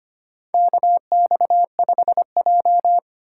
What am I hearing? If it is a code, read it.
Morse code, heard as a keyed tone, about 25 wpm, 700 Hz. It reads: KX5J
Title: USA Call Signs - 4 Characters - 6x Character Spacing - 25wpm